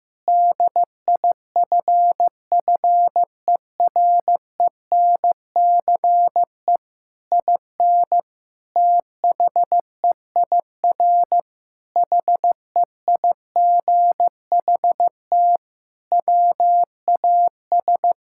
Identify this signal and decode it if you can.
Morse code, heard as a keyed tone, about 15 wpm, 700 Hz. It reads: DIFFERENCE IN THEIR HEIGHT WAS
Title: Sentences from Top 1600 Words 15wpm